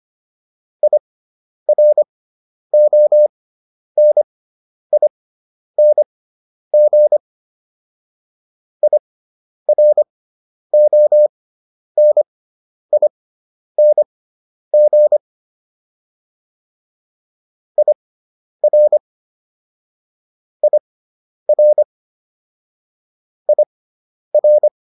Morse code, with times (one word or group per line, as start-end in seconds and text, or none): 0.8-7.2 IRONING
8.8-15.2 IRONING
17.8-19.0 IR
20.6-21.8 IR
23.5-24.7 IR